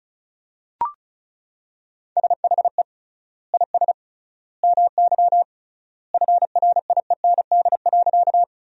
Morse code, heard as a keyed tone, about 34 wpm, 700 Hz. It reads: SHE IS MY FRIEND.
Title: 3x Word-Spacing - Sentences from Top 300 Words - 35wpm